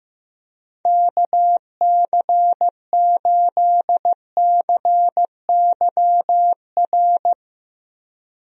Morse code, heard as a keyed tone, about 15 wpm, 700 Hz. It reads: KC8CYR
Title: USA Call Signs - 6 Characters - Repeated 3x - 15wpm